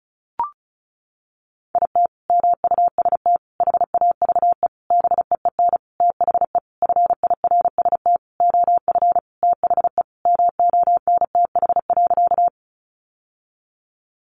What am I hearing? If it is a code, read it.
Morse code, heard as a keyed tone, about 34 wpm, 700 Hz. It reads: IT MUST HAVE BEEN THE FIRST OF THE MONTH.
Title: Sentences from Top 800 Words 35wpm